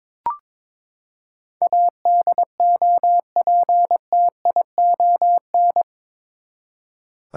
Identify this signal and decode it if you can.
Morse code, heard as a keyed tone, about 22 wpm, 700 Hz. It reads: ADOPTION
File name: practice / 22wpm / Eight Letter Words - 22wpm